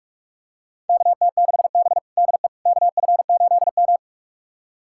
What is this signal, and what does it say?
Morse code, heard as a keyed tone, about 46 wpm, 700 Hz. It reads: KT6B DE KF8M
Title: USA Callsign DE Callsign - 4 Characters - 45wpm